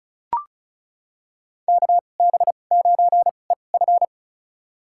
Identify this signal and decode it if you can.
Morse code, heard as a keyed tone, about 36 wpm, spaced as for 25 wpm, 700 Hz. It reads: KB9EF
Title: USA Call Signs - 2x Character Spacing - 35wpm